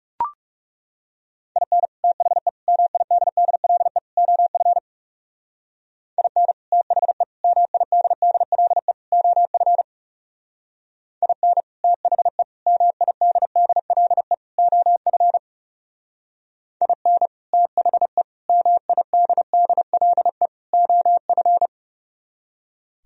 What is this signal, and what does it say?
Morse code, heard as a keyed tone, about 37 wpm, 700 Hz. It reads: IN THE MIDDLE OF IN THE MIDDLE OF IN THE MIDDLE OF IN THE MIDDLE OF
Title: Speed-Racing - Sets of 4 Words 30wpm